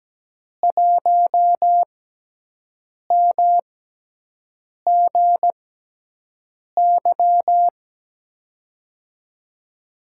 Morse code, heard as a keyed tone, about 17 wpm, spaced as for 6 wpm, 700 Hz. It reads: 1MGY
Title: USA Call Signs - 6x Character Spacing - 17wpm